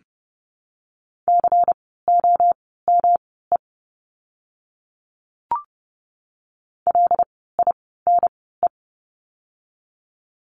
Morse code, heard as a keyed tone, about 30 wpm, spaced as for 17 wpm, 700 Hz. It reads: COME LINE
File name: course / 30:10wpm / 115 Lesson 19f - Letter w - Words 30wpm